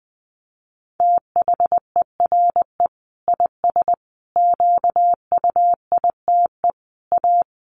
Morse code, heard as a keyed tone, about 20 wpm, 700 Hz. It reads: THERE IS QUITE A
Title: Sentences from Top 800 Words 20wpm